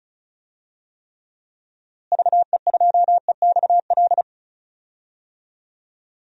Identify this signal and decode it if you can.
Morse code, heard as a keyed tone, about 34 wpm, 700 Hz. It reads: VE2EXL
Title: DXCC-100 Call Signs - 6 Characters - Repeated 3x - 35wpm